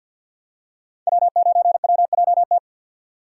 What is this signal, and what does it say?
Morse code, heard as a keyed tone, about 50 wpm, 700 Hz. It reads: W9WJT
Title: Code Groups - 5 Letter:Numbers - 50wpm